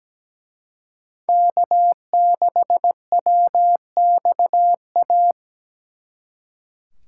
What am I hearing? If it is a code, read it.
Morse code, heard as a keyed tone, about 20 wpm, 700 Hz. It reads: K6WXA
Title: Speed Racing - USA 1x3 Call Signs 17wpm